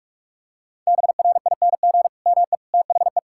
Morse code, heard as a keyed tone, about 46 wpm, 700 Hz. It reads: BRING ME THE
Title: Sentences from Top 1000 Words - Review Part 6 - 45wpm